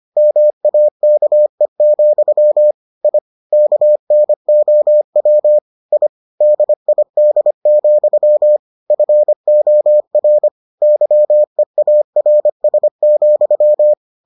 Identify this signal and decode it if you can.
Morse code, heard as a keyed tone, about 25 wpm, 600 Hz. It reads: MAKE, I KNOW I DID, FOR YEARS,